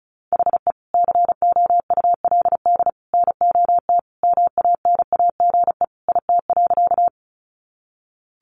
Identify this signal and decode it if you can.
Morse code, heard as a keyed tone, about 34 wpm, 700 Hz. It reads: HE COULD NOT MANAGE IT.
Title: Sentences from Top 1400 Words 35wpm